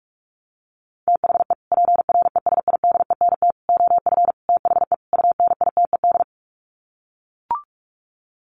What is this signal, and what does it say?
Morse code, heard as a keyed tone, about 44 wpm, 700 Hz. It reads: THE PRESIDENT OF THE UNITED
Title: Sets of 5 Words 45wpm